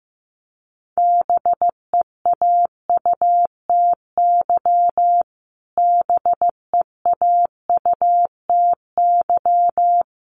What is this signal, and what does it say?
Morse code, heard as a keyed tone, about 15 wpm, 700 Hz. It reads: BEAUTY BEAUTY
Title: Top 500 Words Repeated 15wpm